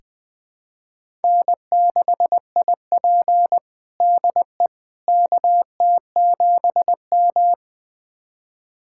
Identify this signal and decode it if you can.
Morse code, heard as a keyed tone, about 21 wpm, 700 Hz. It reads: N6IP DE KT7M
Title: USA Callsign DE Callsign - 4 Characters - 20wpm